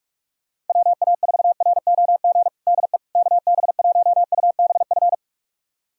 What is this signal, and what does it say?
Morse code, heard as a keyed tone, about 41 wpm, 700 Hz. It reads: WA4ROG DE KB1UBF
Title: USA Callsign DE Callsign - 6 Characters - 45wpm